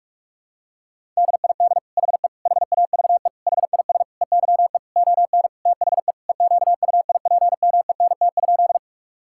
Morse code, heard as a keyed tone, about 46 wpm, 700 Hz. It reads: DID HE HAVE HIS EYE ON THE EQUIPMENT?
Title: Sentences from Top 1000 Words - Review Part 5 - 45wpm